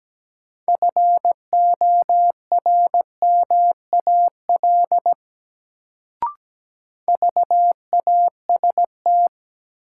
Morse code, heard as a keyed tone, about 17 wpm, 700 Hz. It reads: FORMAL VAST
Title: Sentences from Top 1300 Words - New Word Drill - 17wpm